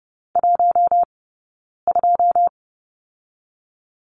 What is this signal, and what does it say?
Morse code, heard as a keyed tone, about 30 wpm, 700 Hz. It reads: 1 2
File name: Lesson 18d - Letter m - Two Characters 30wpm